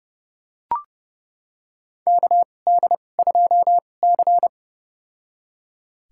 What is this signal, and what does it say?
Morse code, heard as a keyed tone, about 32 wpm, spaced as for 22 wpm, 700 Hz. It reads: KD2C
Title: USA Call Signs - 4 Characters - 2x Character Spacing - 30wpm